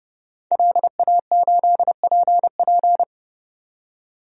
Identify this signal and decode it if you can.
Morse code, heard as a keyed tone, about 30 wpm, 700 Hz. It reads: LA8PP
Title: DXCC-100 Call Signs - 5 Characters - 30wpm